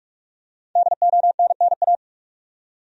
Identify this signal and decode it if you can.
Morse code, heard as a keyed tone, about 46 wpm, 700 Hz. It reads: DONNA
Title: USA Common Names - Max 5 Letters - 45wpm